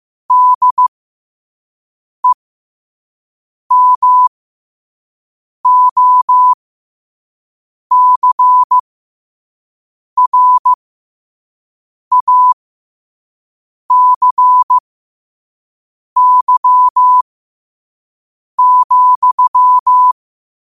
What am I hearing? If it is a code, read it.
Morse code, heard as a keyed tone, about 15 wpm, spaced as for 5 wpm, 1000 Hz. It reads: DEMOCRACY,
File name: qotdfiles / 5wpm